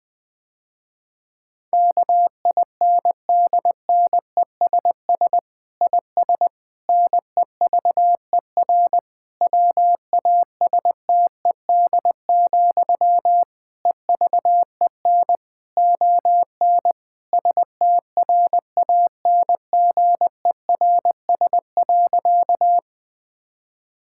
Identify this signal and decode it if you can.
Morse code, heard as a keyed tone, about 20 wpm, 700 Hz. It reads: KINDNESS IS NEVER WASTED, EVEN ON STRANGERS.
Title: Difficult Sentences with Hints - 20wpm